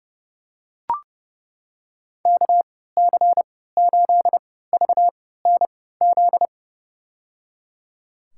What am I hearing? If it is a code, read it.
Morse code, heard as a keyed tone, about 30 wpm, spaced as for 17 wpm, 700 Hz. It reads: KC8VNZ
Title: USA Call Signs - 6 Characters - 3x Character Spacing - 30wpm